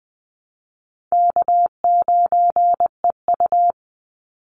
Morse code, heard as a keyed tone, about 20 wpm, 700 Hz. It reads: K9EU
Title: Call Signs - 20wpm